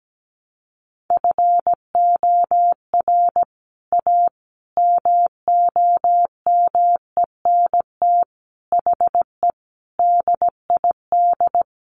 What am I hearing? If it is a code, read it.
Morse code, heard as a keyed tone, about 17 wpm, 700 Hz. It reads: FOR A MOMENT HE DID
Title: Sentences from Top 1100 Words 17wpm